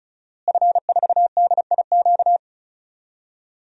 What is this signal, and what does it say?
Morse code, heard as a keyed tone, about 36 wpm, 700 Hz. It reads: F4DIQ
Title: DXCC-100 Call Signs - 5 Characters - Repeated 3x - 35wpm